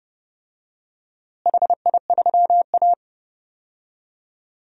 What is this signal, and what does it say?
Morse code, heard as a keyed tone, about 32 wpm, 700 Hz. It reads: HI3A
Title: DXCC-100 Call Signs - 4 Characters - Repeated 2x - 30wpm